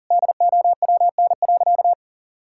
Morse code, heard as a keyed tone, about 40 wpm, 700 Hz. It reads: DOWN.
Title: Sentences from Top 1600 Words 40wpm